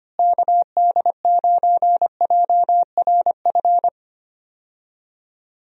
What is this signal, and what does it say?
Morse code, heard as a keyed tone, about 25 wpm, 700 Hz. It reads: KD9JRF
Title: USA Call Signs - 6 Characters - Repeated 3x - 25wpm